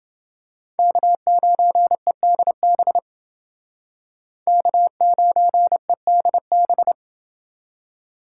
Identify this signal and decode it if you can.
Morse code, heard as a keyed tone, about 29 wpm, 700 Hz. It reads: K9EDB K9EDB